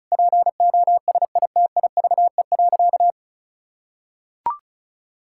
Morse code, heard as a keyed tone, about 36 wpm, 700 Hz. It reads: POSITIVE.
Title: Sentences from Top 1000 Words - Review Part 5 - 35wpm